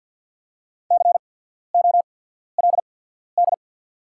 Morse code, heard as a keyed tone, about 48 wpm, spaced as for 14 wpm, 690 Hz. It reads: COLD
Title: 8x Character Spacing - Top 500 Words - 50wpm